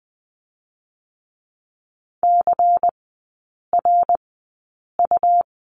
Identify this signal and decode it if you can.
Morse code, heard as a keyed tone, about 20 wpm, 690 Hz. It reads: C R U
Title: 145 Lesson 24e - Letter y - Three Characters 20wpm